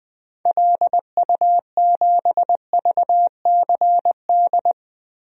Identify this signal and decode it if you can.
Morse code, heard as a keyed tone, about 20 wpm, 700 Hz. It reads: LU7VCD
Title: DXCC-100 Call Signs - 6 Characters - 20wpm